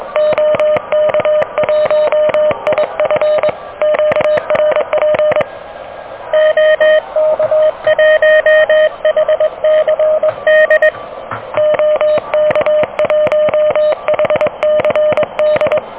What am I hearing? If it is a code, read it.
Morse code, heard as a keyed tone, about 22 wpm, 600 Hz. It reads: OK1IF QRP OK1HCD OK1HCD
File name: OK QRP 2023 / OK QRP 2023